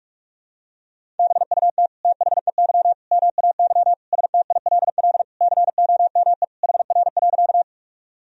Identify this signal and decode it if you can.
Morse code, heard as a keyed tone, about 46 wpm, 700 Hz. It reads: BUT THEY MAY STILL COME HR.
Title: Sentences from Top 300 Words and Abbreviations 45wpm